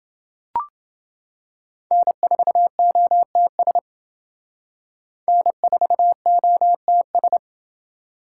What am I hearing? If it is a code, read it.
Morse code, heard as a keyed tone, about 28 wpm, 700 Hz. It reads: N4OTS N4OTS